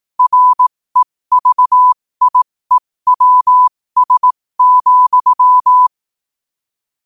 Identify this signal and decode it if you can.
Morse code, heard as a keyed tone, about 18 wpm, spaced as for 15 wpm, 1000 Hz. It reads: REVIEWS,
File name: qotdfiles / 15wpm